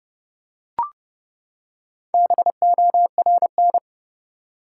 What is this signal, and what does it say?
Morse code, heard as a keyed tone, about 30 wpm, 700 Hz. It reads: BORN
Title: Four Letter Words - 30wpm